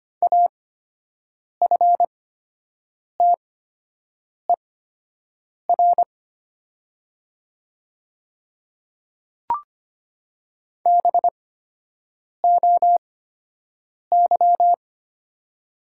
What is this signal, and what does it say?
Morse code, heard as a keyed tone, about 25 wpm, spaced as for 7 wpm, 700 Hz. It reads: AFTER BOY
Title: 8x Character Spacing - Top 200 Words - 25wpm